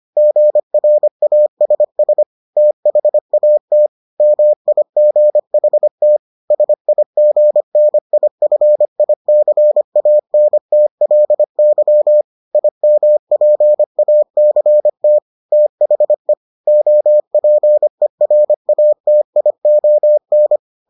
Morse code, heard as a keyed tone, about 25 wpm, 600 Hz. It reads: GRASS THAT MIGHT SIGNIFICANTLY IMPACT THE OPERATION